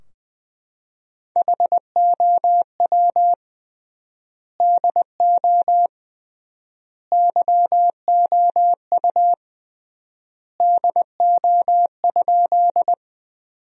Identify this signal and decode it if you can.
Morse code, heard as a keyed tone, about 21 wpm, 700 Hz. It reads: HOW DO YOU DO?
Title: 3x Word-Spacing - Sentences from Top 100 Words - 20wpm